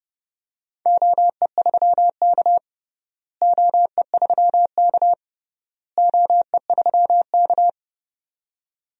Morse code, heard as a keyed tone, about 30 wpm, 700 Hz. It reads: OE3K OE3K OE3K
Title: DXCC-100 Call Signs - 4 Characters - Repeated 3x - 30wpm